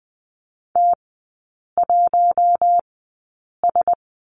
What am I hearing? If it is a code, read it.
Morse code, heard as a keyed tone, about 20 wpm, spaced as for 8 wpm, 700 Hz. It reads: T1S
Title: 041 Lesson 8e - Number 1 - Three Characters 20wpm